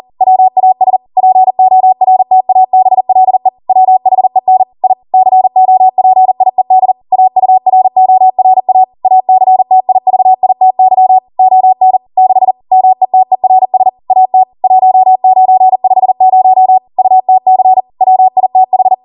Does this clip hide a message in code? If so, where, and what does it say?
Yes: Morse code from 0.2 to 18.9 s, heard as a keyed tone, about 40 wpm, 750 Hz. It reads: WAS PORTABLE WHEN I COPIED AURORA ACTIVITY ON 6 METERS AT 1950 UTC WITH